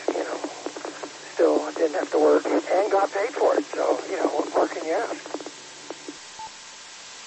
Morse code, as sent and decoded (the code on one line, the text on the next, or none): none